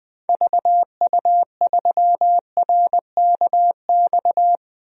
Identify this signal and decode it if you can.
Morse code, heard as a keyed tone, about 20 wpm, 700 Hz. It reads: VU3RKX
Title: DXCC-100 Call Signs - 6 Characters - Repeated 3x - 20wpm